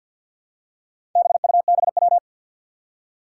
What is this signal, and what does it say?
Morse code, heard as a keyed tone, about 52 wpm, 700 Hz. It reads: BUBW
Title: Four Challenging Letter-Number Combinations 50wpm